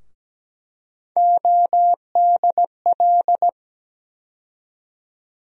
Morse code, heard as a keyed tone, about 17 wpm, 700 Hz. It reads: ODL
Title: USA Call Sign Postfixes - 3 Letters 17wpm